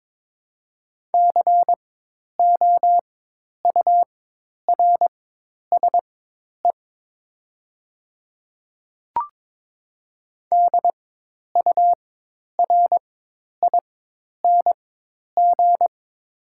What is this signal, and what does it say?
Morse code, heard as a keyed tone, about 22 wpm, spaced as for 10 wpm, 700 Hz. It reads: COURSE DURING
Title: 4x Character Spacing - Top 500 Words - 22wpm